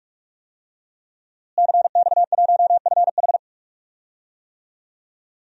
Morse code, heard as a keyed tone, about 46 wpm, 700 Hz. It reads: CX1FH
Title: DXCC-100 Call Signs - 5 Characters - Repeated 2x - 45wpm